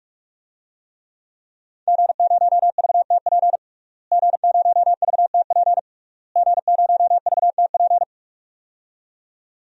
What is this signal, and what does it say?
Morse code, heard as a keyed tone, about 46 wpm, 700 Hz. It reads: G0VTP G0VTP G0VTP
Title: DXCC-100 Call Signs - 5 Characters - Repeated 3x - 45wpm